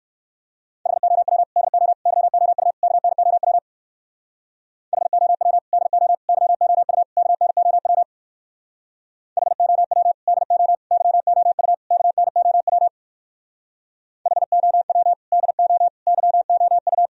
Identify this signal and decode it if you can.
Morse code, heard as a keyed tone, about 61 wpm, 700 Hz. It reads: HOW DO YOU KNOW HOW DO YOU KNOW HOW DO YOU KNOW HOW DO YOU